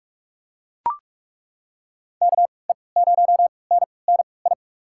Morse code, heard as a keyed tone, about 44 wpm, spaced as for 25 wpm, 700 Hz. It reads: KE0NNI
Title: USA Call Signs - 6 Characters - 3x Character Spacing - 45wpm